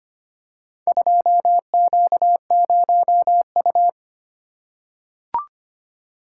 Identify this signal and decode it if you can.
Morse code, heard as a keyed tone, about 25 wpm, 700 Hz. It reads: 2Q0U
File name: practice / 25wpm / Four Letter-Number Combinations - 25wpm